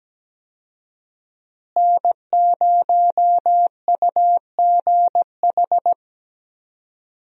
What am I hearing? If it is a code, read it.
Morse code, heard as a keyed tone, about 17 wpm, 700 Hz. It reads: N0UGH
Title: USA Call Signs - 5 Characters - Repeated 3x - 17wpm